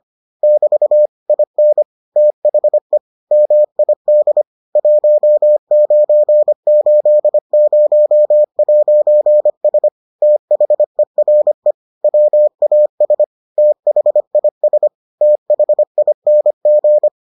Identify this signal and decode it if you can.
Morse code, heard as a keyed tone, about 25 wpm, 600 Hz. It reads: <BT> IN THE MID 1980'S THERE WAS THIS THING